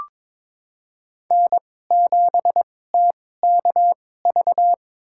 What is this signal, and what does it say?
Morse code, heard as a keyed tone, about 22 wpm, spaced as for 16 wpm, 700 Hz. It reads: N7TKV
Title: USA Call Signs - 5 Characters - 2x Character Spacing - 22wpm